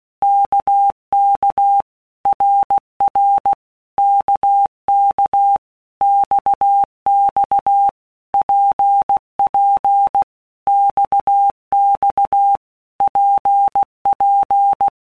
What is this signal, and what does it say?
Morse code, heard as a keyed tone, about 16 wpm, 800 Hz. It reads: KK RR KK XX PP XX PP
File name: Basic Letter Groups By Blocks / Groups@15 / RKPX@15